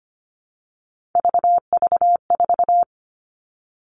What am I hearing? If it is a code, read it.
Morse code, heard as a keyed tone, about 25 wpm, 700 Hz. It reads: VV4
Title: All Cut Numbers - 3-Digit 25wpm